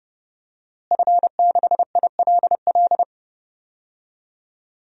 Morse code, heard as a keyed tone, about 30 wpm, 700 Hz. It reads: F6ILL
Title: DXCC-100 Call Signs - 5 Characters - Repeated 2x - 30wpm